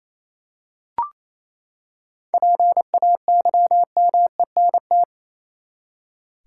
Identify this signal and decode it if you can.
Morse code, heard as a keyed tone, about 28 wpm, 700 Hz. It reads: PAYMENT
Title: Seven Letter Words - 28wpm